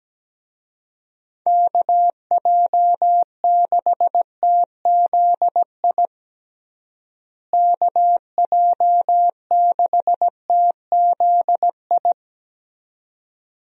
Morse code, heard as a keyed tone, about 17 wpm, 700 Hz. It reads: KJ6TZI KJ6TZI
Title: USA Call Signs - 6 Characters - Repeated 2x - 17wpm